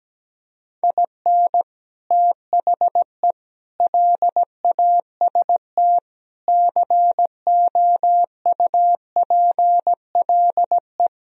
Morse code, heard as a keyed tone, about 17 wpm, 700 Hz. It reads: IN THE LAST COUPLE
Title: Speed-Racing - Sets of 5 Words 17wpm